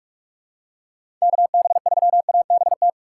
Morse code, heard as a keyed tone, about 41 wpm, 700 Hz. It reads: KB3ABT
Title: USA Call Signs - 3x Character Spacing - 45wpm